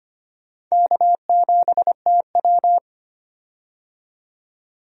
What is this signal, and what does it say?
Morse code, heard as a keyed tone, about 25 wpm, 700 Hz. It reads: K7TW
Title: USA Call Signs - 4 Characters - 25wpm